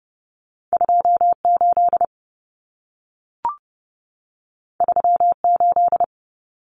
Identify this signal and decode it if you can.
Morse code, heard as a keyed tone, about 30 wpm, 700 Hz. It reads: 28 38
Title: Common Cut Numbers - 2-Digit 30wpm